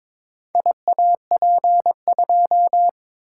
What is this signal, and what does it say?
Morse code, heard as a keyed tone, about 22 wpm, 700 Hz. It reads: IAP2
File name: Four Letter-Number Combinations - 22wpm